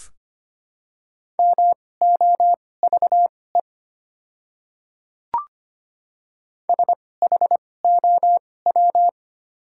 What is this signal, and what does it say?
Morse code, heard as a keyed tone, about 25 wpm, spaced as for 18 wpm, 700 Hz. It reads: MOVE SHOW
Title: 2x Character Spacing - Top 200 Words - 25wpm